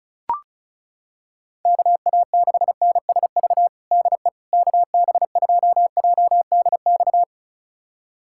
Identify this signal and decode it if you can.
Morse code, heard as a keyed tone, about 36 wpm, 700 Hz. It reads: KA6NSV DE KB2JDX
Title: USA Callsign DE Callsign 35wpm